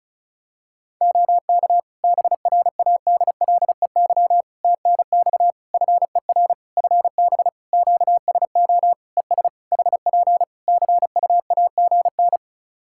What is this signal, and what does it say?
Morse code, heard as a keyed tone, about 34 wpm, 700 Hz. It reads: OK BRADLEY TNX FER FB QSO ES HP CUAGN
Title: Practice QSO - Exchange 4 - Ending - 35wpm